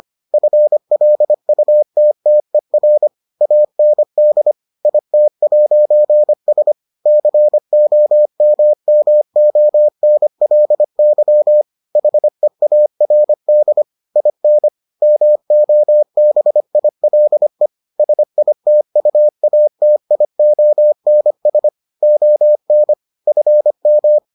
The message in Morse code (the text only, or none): FLUTTER AND IT'S COMMONLY HEARD IN MOBILE SITUATIONS ON FM